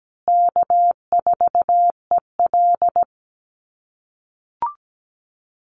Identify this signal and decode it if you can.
Morse code, heard as a keyed tone, about 17 wpm, 700 Hz. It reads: K4EL
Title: USA Call Signs Repeated 3x 17wpm